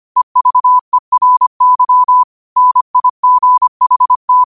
Morse code, heard as a keyed tone, about 25 wpm, 1000 Hz. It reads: EVERY NIGHT